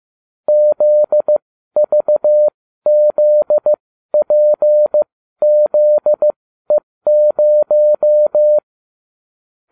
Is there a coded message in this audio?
Yes: Morse code from 0.5 to 8.6 s, heard as a keyed tone, about 15 wpm, spaced as for 12 wpm, 600 Hz. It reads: ZVZPZE0